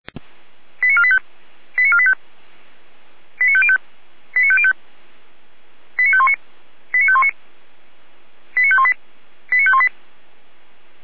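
Five-tone selective call (ZVEI), heard as quick runs of five tones, tones of about 70 ms. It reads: 89485 89485 89505 89505 89519 89519 89518 89518
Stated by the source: ZVEI five-tone selective call received by SDR, fire brigade Feuerwehr Orlamünde.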